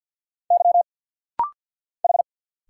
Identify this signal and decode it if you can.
Morse code, heard as a keyed tone, about 50 wpm, 690 Hz. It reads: Y H
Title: Single Letter-Number Alt - Rapid-Fire 50wpm